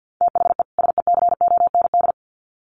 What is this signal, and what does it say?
Morse code, heard as a keyed tone, about 50 wpm, 700 Hz. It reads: THE SECOND
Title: Sets of 2 Words 50wpm